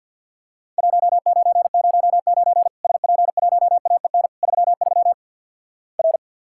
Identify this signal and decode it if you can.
Morse code, heard as a keyed tone, about 48 wpm, 700 Hz. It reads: 1909 SP1AEN 33 R
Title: Worked All Europe DX Contest - QTC Exchange - 50wpm